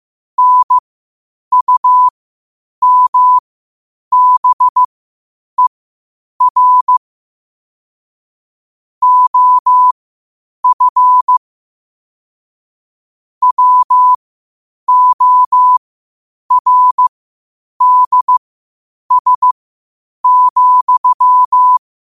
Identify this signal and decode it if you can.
Morse code, heard as a keyed tone, about 15 wpm, spaced as for 8 wpm, 1000 Hz. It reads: NUMBER OF WORDS,
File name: qotdfiles / 8wpm